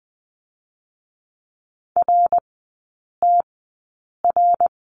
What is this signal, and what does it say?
Morse code, heard as a keyed tone, about 20 wpm, spaced as for 8 wpm, 700 Hz. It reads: RTR
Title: 053 Lesson 10e - Letter r - Three Characters 20wpm